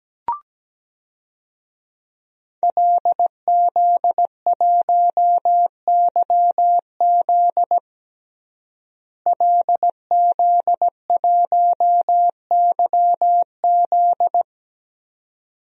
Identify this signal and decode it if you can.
Morse code, heard as a keyed tone, about 17 wpm, 700 Hz. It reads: LZ1YZ LZ1YZ
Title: DXCC-100 Call Signs - 5 Characters - Repeated 2x - 17wpm